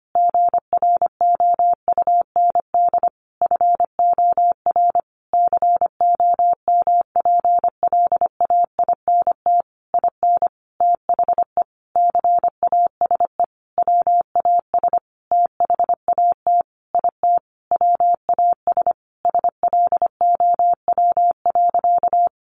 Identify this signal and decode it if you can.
Morse code, heard as a keyed tone, about 25 wpm, 700 Hz. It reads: GROUND FOR COMPLAINT IN THE CASE WAS THAT IT WAS SLOW.
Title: Sentences from Top 1400 Words 25wpm